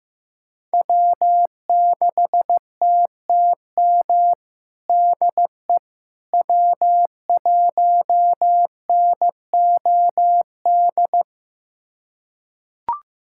Morse code, heard as a keyed tone, about 16 wpm, 700 Hz. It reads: W6TTM DE W1NOD
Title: USA Callsign DE Callsign - 5 Characters - 15wpm